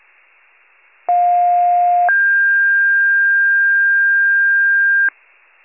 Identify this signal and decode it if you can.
Two-tone page, heard as a one-second tone, then a three-second tone, about 710 then 1600 Hz, with noise. It is heard from 1.1 to 5.1 s.
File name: wcalert